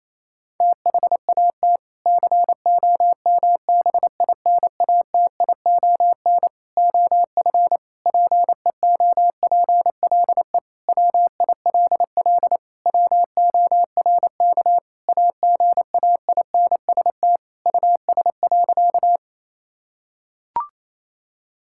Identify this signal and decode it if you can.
Morse code, heard as a keyed tone, about 28 wpm, 700 Hz. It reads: THAT COMBINATION OF PEOPLE WILL WORK AGAINST US.
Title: Sentences from Top 1000 Words - Review Part 6 - 28wpm